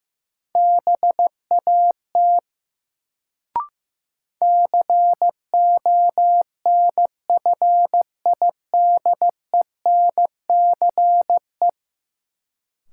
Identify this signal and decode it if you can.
Morse code, heard as a keyed tone, about 15 wpm, 700 Hz. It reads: BAT CONFIDENCE
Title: Sentences from Top 1500 Words - New Word Drill - 15wpm